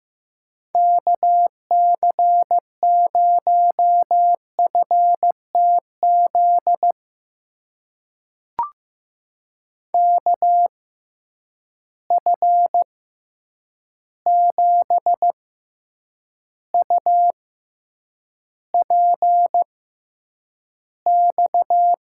Morse code, heard as a keyed tone, about 15 wpm, spaced as for 8 wpm, 700 Hz. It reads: KC0FTZ KF7UPX
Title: USA Call Signs - 6x Character Spacing - 15wpm